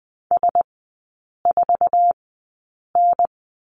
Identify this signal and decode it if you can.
Morse code, heard as a keyed tone, about 20 wpm, spaced as for 8 wpm, 700 Hz. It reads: S4N
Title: 047 Lesson 9e - Number 4 - Three Characters 20wpm